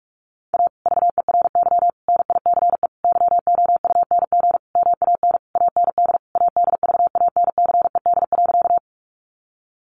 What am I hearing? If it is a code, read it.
Morse code, heard as a keyed tone, about 44 wpm, 700 Hz. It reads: A VERY NICE YOUNG MAN AND ADVANCED.
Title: Sentences from Top 1200 Words 45wpm